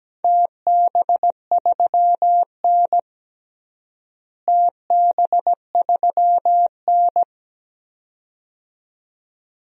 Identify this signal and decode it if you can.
Morse code, heard as a keyed tone, about 17 wpm, 700 Hz. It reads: TB3N TB3N
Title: DXCC-100 Call Signs - 4 Characters - Repeated 2x - 17wpm